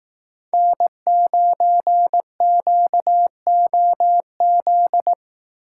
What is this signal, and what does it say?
Morse code, heard as a keyed tone, about 18 wpm, 700 Hz. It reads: N9QOZ
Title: Speed Racing - USA Call Signs 15wpm